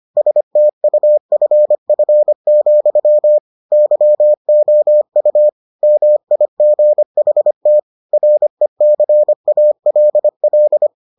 Morse code, heard as a keyed tone, about 25 wpm, 600 Hz. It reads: STUFF, YOU MIGHT RECALL